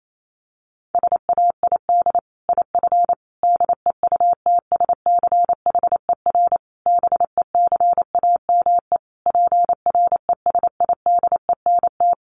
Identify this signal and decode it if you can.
Morse code, heard as a keyed tone, about 28 wpm, 700 Hz. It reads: SAID IF DEUTSCHER BECAME PRESIDENT
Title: A Sound Of Thunder 28wpm